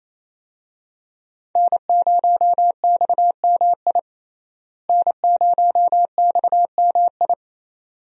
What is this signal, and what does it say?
Morse code, heard as a keyed tone, about 28 wpm, 700 Hz. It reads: N0XMI N0XMI
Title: USA Call Signs - 5 Characters - Repeated 2x - 28wpm